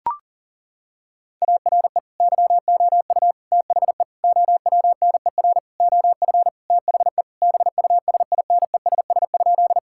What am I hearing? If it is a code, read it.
Morse code, heard as a keyed tone, about 40 wpm, 700 Hz. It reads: ARE YOU THE OWNER OF THE BUSINESS?
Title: Sentences from Top 1600 Words 40wpm